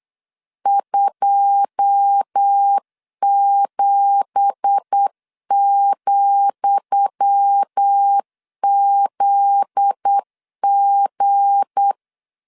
Morse code, heard as a keyed tone, about 8 wpm, 800 Hz. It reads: 27,ZG
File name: LEZIONI / 10 SIGNS / 10D COMA 8wpm 3749